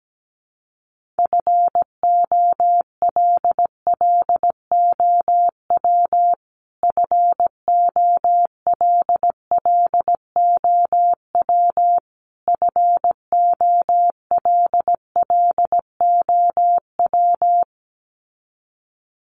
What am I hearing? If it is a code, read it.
Morse code, heard as a keyed tone, about 17 wpm, 700 Hz. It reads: FOLLOW FOLLOW FOLLOW
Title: Top 300 Words Repeated 17wpm